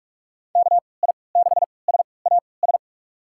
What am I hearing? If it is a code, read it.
Morse code, heard as a keyed tone, about 46 wpm, spaced as for 25 wpm, 700 Hz. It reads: KI6SAS
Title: USA Call Signs - 6 Characters - 3x Character Spacing - 45wpm